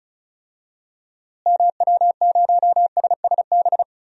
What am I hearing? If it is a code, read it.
Morse code, heard as a keyed tone, about 36 wpm, 700 Hz. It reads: MW0SSB
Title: DXCC-100 Call Signs - 6 Characters - Repeated 3x - 35wpm